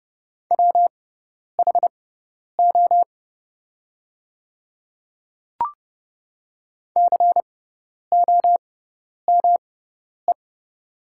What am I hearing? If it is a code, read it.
Morse code, heard as a keyed tone, about 29 wpm, spaced as for 10 wpm, 700 Hz. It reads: WHO COME
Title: 6x Character Spacing - Top 200 Words - 30wpm